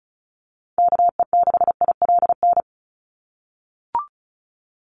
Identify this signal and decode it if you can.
Morse code, heard as a keyed tone, about 34 wpm, 700 Hz. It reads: KE6ILN
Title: USA Call Signs 35wpm